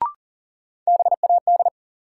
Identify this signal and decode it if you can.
Morse code, heard as a keyed tone, about 40 wpm, 700 Hz. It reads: BAD
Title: Sentences from Top 700 Words - New Word Drill - 40wpm